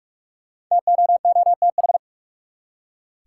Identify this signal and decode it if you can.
Morse code, heard as a keyed tone, about 47 wpm, 700 Hz. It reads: TOOTH
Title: Five Letter Words - 45wpm